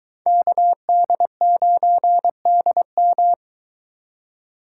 Morse code, heard as a keyed tone, about 22 wpm, 700 Hz. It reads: KD9DM